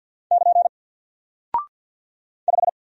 Morse code, heard as a keyed tone, about 49 wpm, 700 Hz. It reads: <KN> 5
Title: Single Character (Letter:Number:Punctuation:Prosign) 50wpm